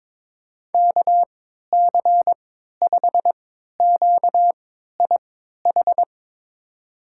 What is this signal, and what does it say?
Morse code, heard as a keyed tone, about 22 wpm, spaced as for 13 wpm, 700 Hz. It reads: KC5QIH
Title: USA Call Signs - 6 Characters - 3x Character Spacing - 22wpm